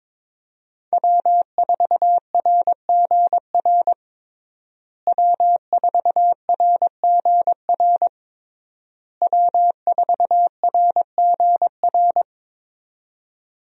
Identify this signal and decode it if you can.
Morse code, heard as a keyed tone, about 21 wpm, 700 Hz. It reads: W4RGR W4RGR W4RGR
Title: USA Call Signs - 5 Characters - Repeated 3x - 22wpm